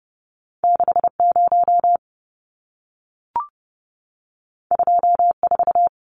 Morse code, heard as a keyed tone, about 30 wpm, 700 Hz. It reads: 60 24
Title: Numbers - 2-Digit 30wpm